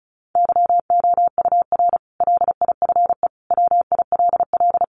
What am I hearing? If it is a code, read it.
Morse code, heard as a keyed tone, about 34 wpm, 700 Hz. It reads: YOUR LIFE WILL
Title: Sentences from Top 1300 Words 35wpm